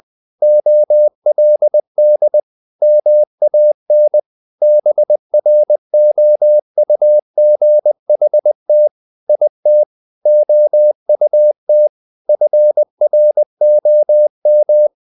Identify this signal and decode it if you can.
Morse code, heard as a keyed tone, about 20 wpm, 600 Hz. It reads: OLD MAN BROUGHT IT OUT FROM
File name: the old man and the sea / the old man and the sea part 1 20wpm